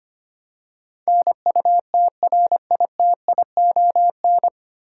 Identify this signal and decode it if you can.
Morse code, heard as a keyed tone, about 25 wpm, 700 Hz. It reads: NUTRITION
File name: Nine Letter Words - 25wpm